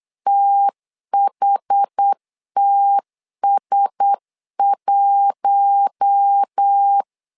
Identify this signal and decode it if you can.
Morse code, heard as a keyed tone, about 8 wpm, 800 Hz. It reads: THTS1
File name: LEZIONI / 4 Q1S3 8wpm 3037